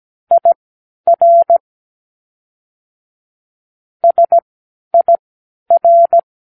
Morse code, heard as a keyed tone, about 17 wpm, spaced as for 11 wpm, 680 Hz. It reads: IR SIR
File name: k7qo / Practicewords